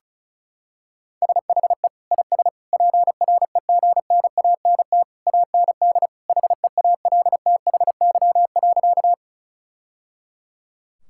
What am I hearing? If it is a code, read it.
Morse code, heard as a keyed tone, about 36 wpm, 700 Hz. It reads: SHE IS PREGNANT AND HEALTHY.
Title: Sentences from Top 1000 Words - Review Part 6 - 35wpm